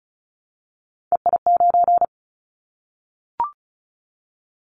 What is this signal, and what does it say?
Morse code, heard as a keyed tone, about 36 wpm, 700 Hz. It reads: EI9
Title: Callsign Prefixes for Top 50 DXCC - 35wpm